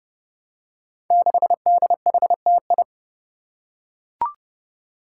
Morse code, heard as a keyed tone, about 29 wpm, 700 Hz. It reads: 6DHTI
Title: Five Challenging Letter-Number Combinations 30wpm